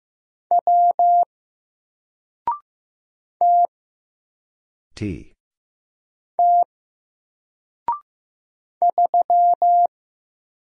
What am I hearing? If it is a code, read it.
Morse code, heard as a keyed tone, about 15 wpm, 700 Hz. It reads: W T T 3
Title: Single Letter-Number 15wpm